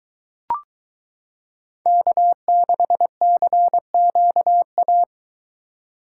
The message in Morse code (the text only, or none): K6CQA